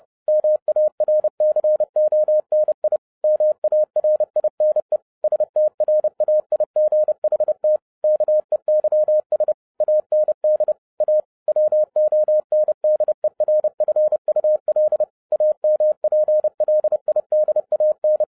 Morse code, heard as a keyed tone, about 30 wpm, 610 Hz. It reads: MARCONI MARINE STRAIGHT KEYS AND A WONDERFUL AMPLIDAN
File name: 2011-07-mp3 / n20 GREETINGS FROM SINGAPORE 30wpm